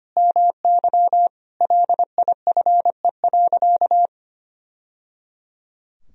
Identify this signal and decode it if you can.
Morse code, heard as a keyed tone, about 25 wpm, 700 Hz. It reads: MY LIFE.
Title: Sentences from Top 1000 Words - Review Part 1 - 25wpm